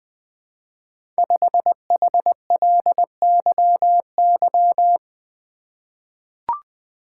Morse code, heard as a keyed tone, about 20 wpm, 700 Hz. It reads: 5HLYY
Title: Code Groups - 5 Letter:Numbers - 20wpm